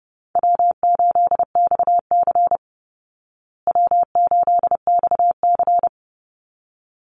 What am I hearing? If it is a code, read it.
Morse code, heard as a keyed tone, about 30 wpm, 700 Hz. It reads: W8XC W8XC
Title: Easy Call Signs Repeated 2x - 30wpm